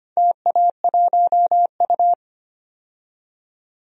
Morse code, heard as a keyed tone, about 25 wpm, 700 Hz. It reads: TA1U
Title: DXCC-100 Call Signs - 4 Characters - 25wpm